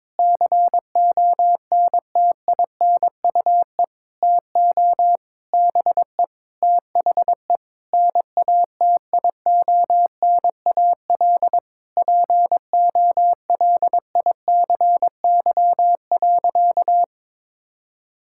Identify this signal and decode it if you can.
Morse code, heard as a keyed tone, about 22 wpm, 700 Hz. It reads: CONTINUE TO BE THE NATIONAL POLICY.
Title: Sentences from Top 1000 Words - Review Part 6 - 22wpm